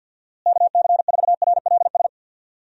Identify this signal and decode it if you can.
Morse code, heard as a keyed tone, about 49 wpm, 700 Hz. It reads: KC4RLS
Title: USA Call Signs - 6 Characters - Repeated 2x - 50wpm